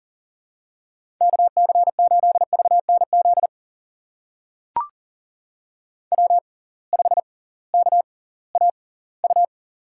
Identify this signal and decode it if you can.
Morse code, heard as a keyed tone, about 40 wpm, spaced as for 22 wpm, 700 Hz. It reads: KC8VNZ W5KAU
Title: USA Call Signs - 6x Character Spacing - 40wpm